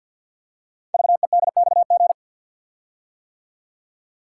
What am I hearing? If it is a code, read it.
Morse code, heard as a keyed tone, about 48 wpm, 700 Hz. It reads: VEDXG
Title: Code Groups - 5 Letters - 50wpm